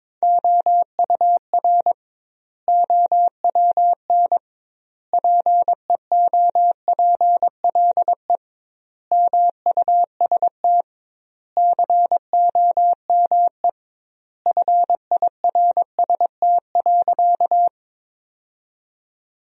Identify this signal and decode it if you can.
Morse code, heard as a keyed tone, about 22 wpm, 700 Hz. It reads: OUR OWN PEOPLE MUST COME FIRST.
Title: 2x Word-Spacing - Sentences from Top 300 Words - 22wpm